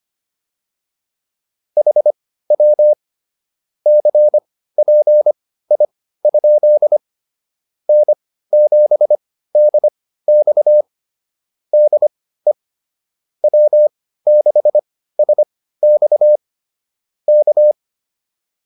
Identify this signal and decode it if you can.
Morse code, heard as a keyed tone, about 25 wpm, spaced as for 15 wpm, 600 Hz. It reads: HW CPI? N7DX DE W6SX K
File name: QSO102 15